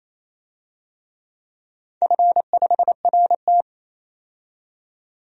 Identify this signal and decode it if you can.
Morse code, heard as a keyed tone, about 27 wpm, 700 Hz. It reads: F5RT
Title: DXCC-100 Call Signs - 4 Characters - 28wpm